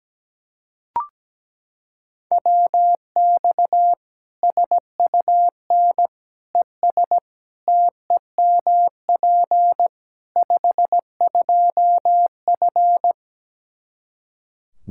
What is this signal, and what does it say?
Morse code, heard as a keyed tone, about 17 wpm, 700 Hz. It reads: WX SUN ES TEMP 52F
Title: Practice QSO - Exchange 2 - Info - 17wpm